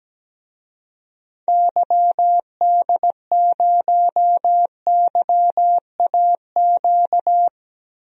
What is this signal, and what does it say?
Morse code, heard as a keyed tone, about 17 wpm, 700 Hz. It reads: YD0YAQ
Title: DXCC-100 Call Signs - 6 Characters - 17wpm